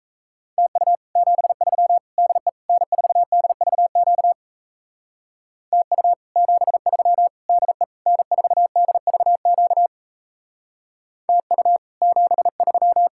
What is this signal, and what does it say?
Morse code, heard as a keyed tone, about 34 wpm, 700 Hz. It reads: TU 73 DE N4DVQ TU 73 DE N4DVQ TU 73
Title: Speed-Racing Common QSO Phrases - 28wpm